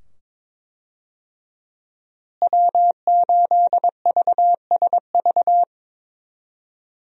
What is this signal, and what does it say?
Morse code, heard as a keyed tone, about 22 wpm, 700 Hz. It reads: W8VSV
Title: USA Call Signs - 5 Characters - Repeated 2x - 22wpm